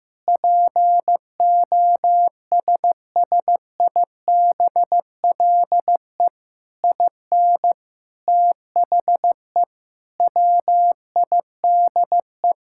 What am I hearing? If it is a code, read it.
Morse code, heard as a keyed tone, about 14 wpm, 700 Hz. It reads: POSSIBLE IN THE WIDE
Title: Sentences from Top 1000 Words - Review Part 6 - 15wpm